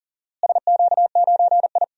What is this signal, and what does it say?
Morse code, heard as a keyed tone, about 40 wpm, 700 Hz. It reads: SQ9I
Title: DXCC-100 Call Signs - 4 Characters - Repeated 3x - 40wpm